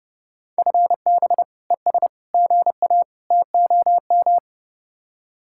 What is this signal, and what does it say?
Morse code, heard as a keyed tone, about 37 wpm, 700 Hz. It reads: FB ES GA TOM
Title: Speed-Racing Common QSO Phrases - 30wpm